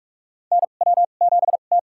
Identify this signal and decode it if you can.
Morse code, heard as a keyed tone, about 45 wpm, spaced as for 32 wpm, 700 Hz. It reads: NW7T
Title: USA Call Signs - 4 Characters - 2x Character Spacing - 45wpm